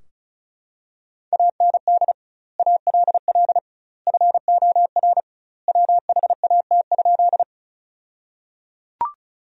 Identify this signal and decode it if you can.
Morse code, heard as a keyed tone, about 36 wpm, 700 Hz. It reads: AND ALL FOR WHAT?
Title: 2x Word-Spacing - Sentences from Top 100 Words - 35wpm